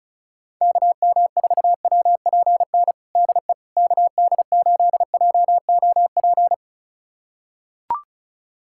Morse code, heard as a keyed tone, about 36 wpm, 700 Hz. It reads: KM4WPN DE KD8JOP
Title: USA Callsign DE Callsign - 6 Characters - 35wpm